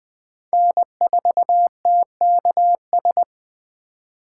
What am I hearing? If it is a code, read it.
Morse code, heard as a keyed tone, about 24 wpm, 700 Hz. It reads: N4TKS
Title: Speed Racing - USA Call Signs 20wpm